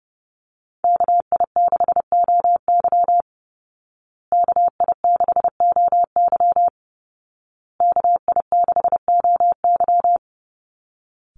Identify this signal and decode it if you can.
Morse code, heard as a keyed tone, about 30 wpm, 700 Hz. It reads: KI6OY KI6OY KI6OY
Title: Call Signs Repeated 2x - 30wpm